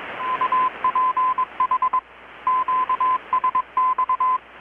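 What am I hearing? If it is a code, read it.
Morse code, heard as a keyed tone, about 21 wpm, 1020 Hz, with radio noise. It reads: UPH QSX
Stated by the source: Morse from KPH's Night of Nights XX maritime radiotelegraphy event, received in Chiapas, Mexico about 3,800 km away on a Kenwood R-600 with a grounded 10 m long-wire antenna.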